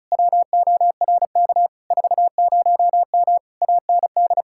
Morse code, heard as a keyed tone, about 43 wpm, 700 Hz. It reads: WORK 40M AND
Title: Speed-Racing Common QSO Phrases - 35wpm